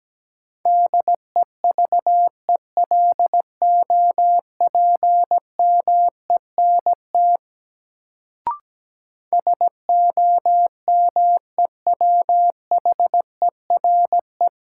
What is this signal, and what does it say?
Morse code, heard as a keyed tone, about 17 wpm, 700 Hz. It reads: DEVELOPMENT SOMEWHERE
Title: Sentences from Top 1400 Words - New Word Drill Alt - 17wpm